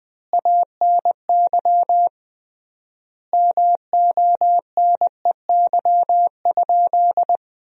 Morse code, heard as a keyed tone, about 20 wpm, 700 Hz. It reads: ANY MONEY?
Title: 3x Word-Spacing - Sentences from Top 400 Words - 20wpm